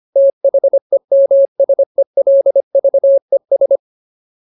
Morse code, heard as a keyed tone, about 25 wpm, 550 Hz. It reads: THEMSELVES
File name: Ultimate Word List / 25WPM Ultimate